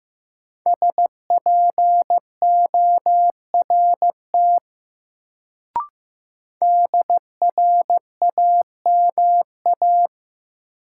Morse code, heard as a keyed tone, about 15 wpm, 700 Hz. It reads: SPORT DRAMA
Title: Sentences from Top 1500 Words - New Word Drill Alt - 15wpm